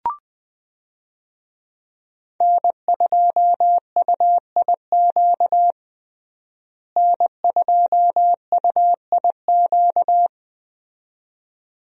Morse code, heard as a keyed tone, about 20 wpm, 700 Hz. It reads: N2UIQ N2UIQ
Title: USA Call Signs - 5 Characters - Repeated 2x - 20wpm